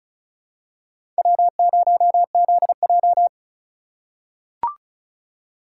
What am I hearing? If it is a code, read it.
Morse code, heard as a keyed tone, about 35 wpm, 700 Hz. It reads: W0ZJ
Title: USA Call Signs - 4 Characters - 3x Character Spacing - 35wpm